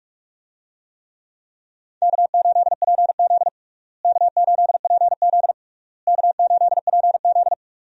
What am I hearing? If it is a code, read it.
Morse code, heard as a keyed tone, about 46 wpm, 700 Hz. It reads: K8PZ K8PZ K8PZ
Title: USA Call Signs - 4 Characters - Repeated 3x - 45wpm